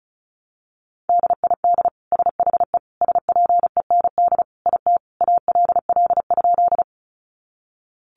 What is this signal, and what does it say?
Morse code, heard as a keyed tone, about 34 wpm, 700 Hz. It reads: DID SHE SPEND IT ALL?
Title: Sentences from Top 800 Words 35wpm